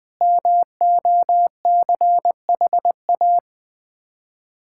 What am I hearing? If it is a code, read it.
Morse code, heard as a keyed tone, about 20 wpm, 700 Hz. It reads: MOCHA
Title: Difficult Words with Hints - 20wpm